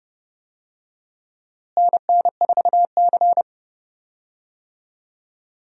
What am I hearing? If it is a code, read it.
Morse code, heard as a keyed tone, about 30 wpm, 700 Hz. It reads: NN4C
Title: USA Call Signs - 4 Characters - Repeated 3x - 30wpm